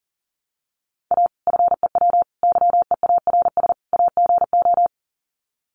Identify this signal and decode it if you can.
Morse code, heard as a keyed tone, about 40 wpm, 700 Hz. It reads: A FEW YEARS AGO
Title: Sets of 4 Words 40wpm